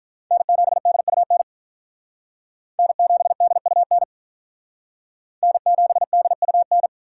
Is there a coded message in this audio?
Yes: Morse code from 0.3 to 6.9 s, heard as a keyed tone, about 45 wpm, 700 Hz. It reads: N7DUN N7DUN N7DUN